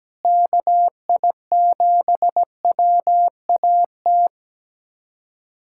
Morse code, heard as a keyed tone, about 17 wpm, 700 Hz. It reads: KI7WAT
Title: USA Callsign DE Callsign 17wpm